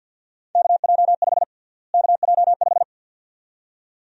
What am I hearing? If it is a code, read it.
Morse code, heard as a keyed tone, about 46 wpm, 700 Hz. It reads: KJ5 KJ5
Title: USA Call Sign Prefixes - Repeated 2x - 50wpm